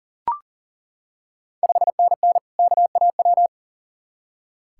Morse code, heard as a keyed tone, about 40 wpm, 690 Hz. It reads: 5NN KAW
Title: Ontario QSO Party Exchange 40wpm